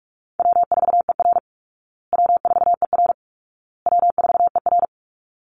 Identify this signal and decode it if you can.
Morse code, heard as a keyed tone, about 44 wpm, 700 Hz. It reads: W4ER W4ER W4ER
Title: Easy Call Signs Repeated 3x - 45wpm